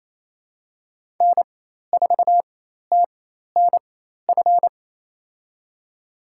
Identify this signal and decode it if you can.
Morse code, heard as a keyed tone, about 27 wpm, spaced as for 13 wpm, 700 Hz. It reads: N4TNF
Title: USA Call Signs - 5 Characters - 4x Character Spacing - 28wpm